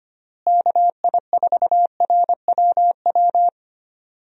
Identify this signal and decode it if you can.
Morse code, heard as a keyed tone, about 25 wpm, 700 Hz. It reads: KI4RWW
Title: USA Call Signs - 6 Characters - Repeated 2x - 25wpm